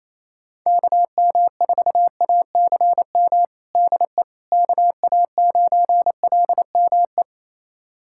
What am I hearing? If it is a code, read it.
Morse code, heard as a keyed tone, about 28 wpm, 700 Hz. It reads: KM4ACM DE KA9LME
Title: USA Callsign DE Callsign 28wpm